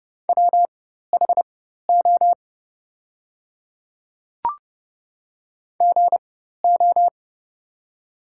Morse code, heard as a keyed tone, about 30 wpm, spaced as for 14 wpm, 700 Hz. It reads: WHO GO
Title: 4x Character Spacing - Top 100 Words - 30wpm